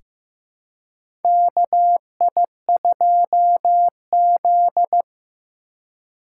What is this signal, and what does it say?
Morse code, heard as a keyed tone, about 15 wpm, 700 Hz. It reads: KI2Z
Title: USA Call Signs - 4 Characters - 4x Character Spacing - 15wpm